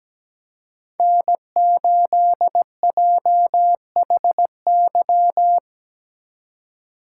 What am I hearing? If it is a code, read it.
Morse code, heard as a keyed tone, about 19 wpm, 700 Hz. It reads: N8JHY
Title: Speed Racing - USA 1x3 Call Signs 17wpm